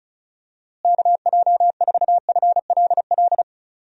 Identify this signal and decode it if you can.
Morse code, heard as a keyed tone, about 36 wpm, 700 Hz. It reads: KJ4FLL
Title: USA Call Signs - 6 Characters - Repeated 3x - 35wpm